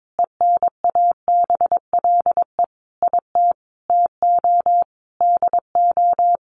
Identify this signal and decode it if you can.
Morse code, heard as a keyed tone, about 22 wpm, 700 Hz. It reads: ENABLE IT TO DO
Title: Sentences from Top 1500 Words 22wpm